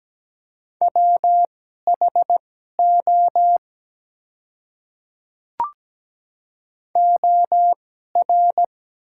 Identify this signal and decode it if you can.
Morse code, heard as a keyed tone, about 17 wpm, spaced as for 12 wpm, 700 Hz. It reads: WHO OR
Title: 2x Character Spacing - Top 100 Words - 17wpm